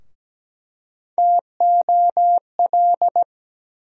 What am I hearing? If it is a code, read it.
Morse code, heard as a keyed tone, about 17 wpm, 700 Hz. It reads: TOL
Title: USA Call Sign Postfixes - 3 Letters 17wpm